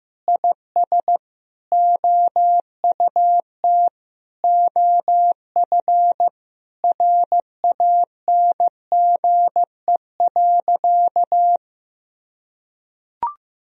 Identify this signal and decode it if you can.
Morse code, heard as a keyed tone, about 15 wpm, 700 Hz. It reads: IS OUT OF RANGE.
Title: Sentences from Top 1000 Words - Review Part 4 - 15wpm